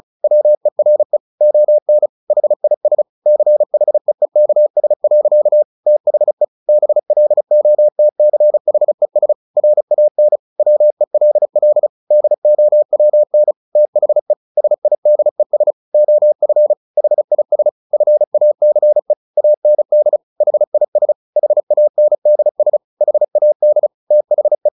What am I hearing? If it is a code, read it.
Morse code, heard as a keyed tone, about 34 wpm, 600 Hz. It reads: WERE ON HIS CHEEKS. THE BLOTCHES RAN WELL DOWN THE SIDES OF HIS FACE AND HIS HANDS HAD THE
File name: the old man and the sea / the old man and the sea 1 35wpm